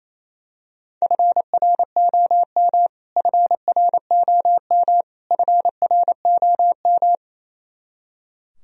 Morse code, heard as a keyed tone, about 28 wpm, 700 Hz. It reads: FROM FROM FROM
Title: Top 100 Words Repeated 28wpm